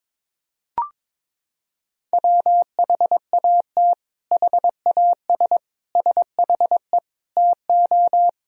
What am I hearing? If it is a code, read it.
Morse code, heard as a keyed tone, about 22 wpm, 700 Hz. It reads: WHAT HAS SHE TO
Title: Sentences from Top 100 Words and Abbreviations 22wpm